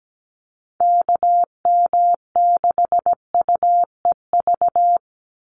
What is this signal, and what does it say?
Morse code, heard as a keyed tone, about 17 wpm, 700 Hz. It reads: KM6UEV
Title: USA Call Signs 17wpm